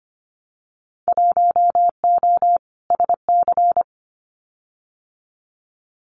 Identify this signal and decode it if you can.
Morse code, heard as a keyed tone, about 25 wpm, 700 Hz. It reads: 1O SC
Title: Winter Field Day Exchanges 25wpm